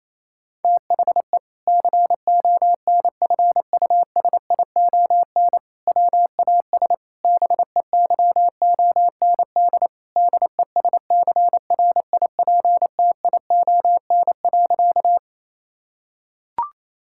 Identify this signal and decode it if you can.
Morse code, heard as a keyed tone, about 28 wpm, 700 Hz. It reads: THE CONFUSION WAS BEYOND DESCRIPTION.
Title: Sentences from Top 1600 Words 28wpm